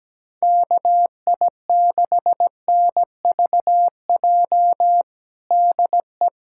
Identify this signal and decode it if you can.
Morse code, heard as a keyed tone, about 17 wpm, 700 Hz. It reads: KI6NVJ DE
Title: USA Callsign DE Callsign - 6 Characters - 17wpm